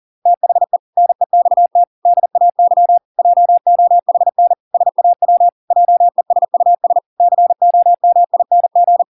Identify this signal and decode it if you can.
Morse code, heard as a keyed tone, about 40 wpm, 700 Hz. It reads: THE NEXT DAY JOHN SAW JESUS COMING